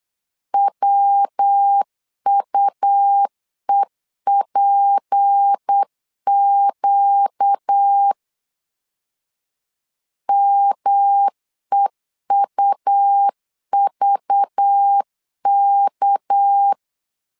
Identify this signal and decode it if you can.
Morse code, heard as a keyed tone, about 8 wpm, 800 Hz. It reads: WUEPQ MEUVK
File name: LEZIONI / 6 PKX0 / 6 PK 8wpm 3003